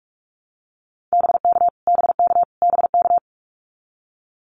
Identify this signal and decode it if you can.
Morse code, heard as a keyed tone, about 44 wpm, 700 Hz. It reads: BK BK BK
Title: Abbreviations Repeated 3x - 45wpm